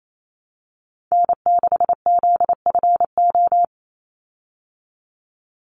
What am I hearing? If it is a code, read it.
Morse code, heard as a keyed tone, about 28 wpm, 690 Hz. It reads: N6ZFO
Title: Call Signs Repeated 3x - 28wpm